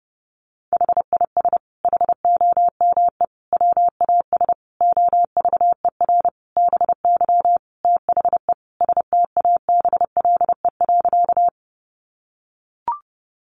Morse code, heard as a keyed tone, about 28 wpm, 700 Hz. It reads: HIS HOME WAS OVER BY THE STABLE.
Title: Sentences from Top 1500 Words 30wpm